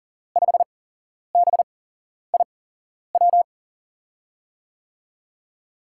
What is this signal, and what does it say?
Morse code, heard as a keyed tone, about 41 wpm, spaced as for 11 wpm, 700 Hz. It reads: 5BIW
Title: USA Call Signs - 6 Characters - 8x Character Spacing - 40wpm